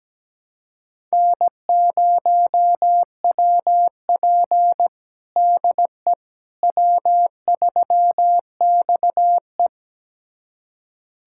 Morse code, heard as a keyed tone, about 17 wpm, 700 Hz. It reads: N0WP DE W3XE
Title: USA Callsign DE Callsign - 4 Characters - 17wpm